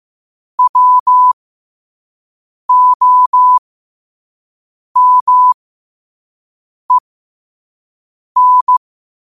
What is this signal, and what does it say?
Morse code, heard as a keyed tone, about 15 wpm, spaced as for 5 wpm, 1000 Hz. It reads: WOMEN
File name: qotdfiles / 5wpm